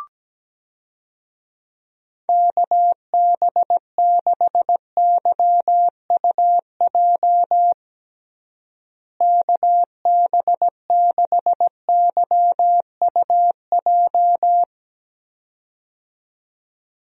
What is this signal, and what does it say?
Morse code, heard as a keyed tone, about 17 wpm, 700 Hz. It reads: KB6YUJ KB6YUJ
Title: USA Call Signs - 6 Characters - Repeated 2x - 17wpm